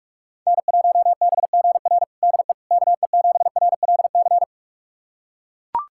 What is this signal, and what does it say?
Morse code, heard as a keyed tone, about 42 wpm, 700 Hz. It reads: N1BGR DE KE7RLC
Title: USA Callsign DE Callsign 45wpm